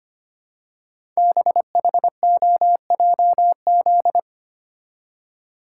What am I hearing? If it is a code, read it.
Morse code, heard as a keyed tone, about 25 wpm, 700 Hz. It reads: BHOJZ
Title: Code Groups - 5 Letters - 25wpm